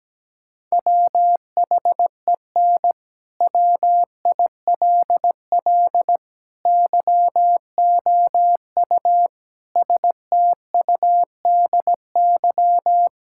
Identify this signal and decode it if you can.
Morse code, heard as a keyed tone, about 17 wpm, 700 Hz. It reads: WHEN WILL YOU STUDY
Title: Sentences from Top 1000 Words - Review Part 6 - 17wpm